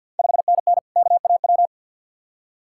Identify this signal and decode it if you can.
Morse code, heard as a keyed tone, about 51 wpm, 690 Hz. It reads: 5NN KAW
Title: Ontario QSO Party Exchange 50wpm